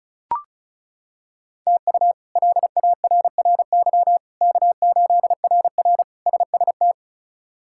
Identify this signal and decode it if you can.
Morse code, heard as a keyed tone, about 34 wpm, 700 Hz. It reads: TU LARRY K8RR SST
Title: K1USN SST QSO Practice - 35wpm